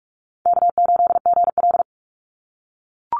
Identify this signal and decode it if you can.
Morse code, heard as a keyed tone, about 44 wpm, 700 Hz. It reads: K8GL
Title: Call Signs - 45wpm